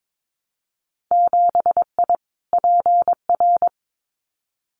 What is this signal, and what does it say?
Morse code, heard as a keyed tone, about 22 wpm, 700 Hz. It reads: 7I PR
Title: Winter Field Day Exchanges Encore 22wpm